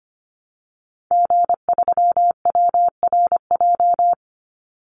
Morse code, heard as a keyed tone, about 25 wpm, 700 Hz. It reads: G3WRJ
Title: Call Signs - 25wpm